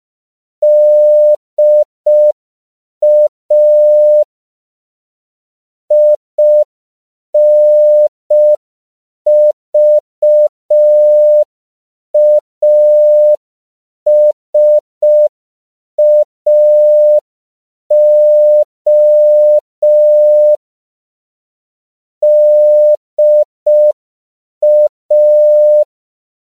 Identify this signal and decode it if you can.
Morse code, heard as a keyed tone, about 5 wpm, 600 Hz. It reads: DA INVASAO DA
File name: CW / 5PPM